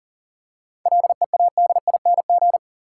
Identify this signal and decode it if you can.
Morse code, heard as a keyed tone, about 40 wpm, 700 Hz. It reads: LEADING
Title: Seven Letter Words - 40wpm